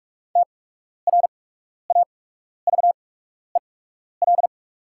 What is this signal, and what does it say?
Morse code, heard as a keyed tone, about 44 wpm, spaced as for 12 wpm, 690 Hz. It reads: TRAVEL
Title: 8x Character Spacing - Top 500 Words - 45wpm